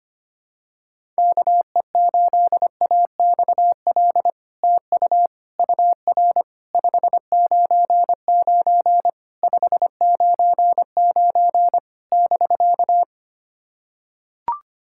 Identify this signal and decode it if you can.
Morse code, heard as a keyed tone, about 25 wpm, 700 Hz. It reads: KE8AXL TU UR 599 599 <BK>
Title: SOTA QSO Practice 25wpm